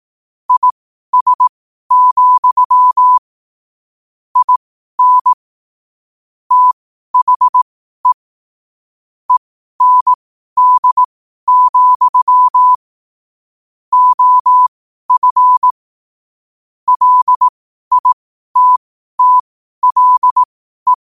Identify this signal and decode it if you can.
Morse code, heard as a keyed tone, about 18 wpm, spaced as for 12 wpm, 1000 Hz. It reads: IS, IN THE END, OF LITTLE
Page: New quotes every day in morse code at 12 Words per minute.